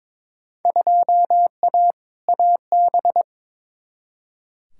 Morse code, heard as a keyed tone, about 22 wpm, 700 Hz. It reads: 2A AB
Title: Field Day Exchanges - 22wpm